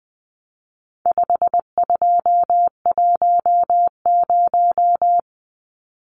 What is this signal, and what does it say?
Morse code, heard as a keyed tone, about 20 wpm, 700 Hz. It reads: 5210
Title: Numbers - 4-Digit 20wpm